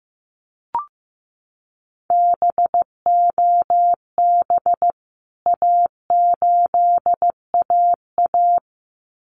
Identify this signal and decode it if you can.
Morse code, heard as a keyed tone, about 15 wpm, 700 Hz. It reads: BOB A8AA
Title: CWT Exchange 15wpm